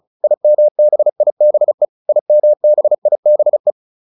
Morse code, heard as a keyed tone, about 34 wpm, 600 Hz. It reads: IMBIBE IMBIBE